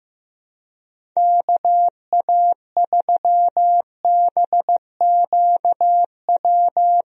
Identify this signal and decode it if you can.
Morse code, heard as a keyed tone, about 15 wpm, 700 Hz. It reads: KA3BQW
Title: USA Call Signs - 6 Characters - 15wpm